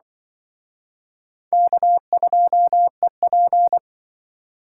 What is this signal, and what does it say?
Morse code, heard as a keyed tone, about 24 wpm, 700 Hz. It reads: K2EP
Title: Speed Racing - USA 1x2 Call Signs 20wpm